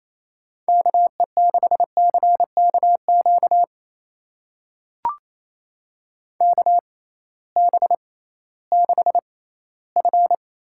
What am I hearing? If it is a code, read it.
Morse code, heard as a keyed tone, about 28 wpm, spaced as for 16 wpm, 700 Hz. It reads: KE6CKQ KB6F
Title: USA Call Signs - 6 Characters - 6x Character Spacing - 28wpm